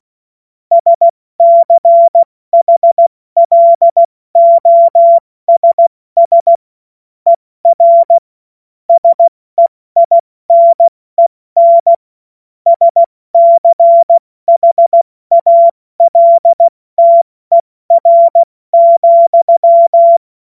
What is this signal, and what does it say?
Morse code, heard as a keyed tone, about 16 wpm, spaced as for 14 wpm, 680 Hz. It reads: SCHLOSS ER SEINEN SCHALTER,
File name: Morsekurs / Witze / Witz 16 Januar 2024 70 F80